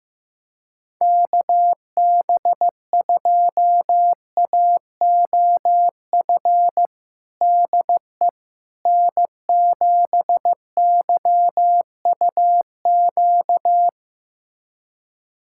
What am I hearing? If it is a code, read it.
Morse code, heard as a keyed tone, about 16 wpm, 700 Hz. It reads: KB2AOF DE N7YUQ
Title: USA Callsign DE Callsign 15wpm